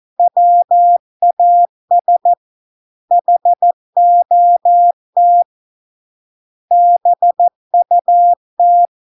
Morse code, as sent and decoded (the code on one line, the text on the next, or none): .-- .- ... / .... --- - / -... ..- -
WAS HOT BUT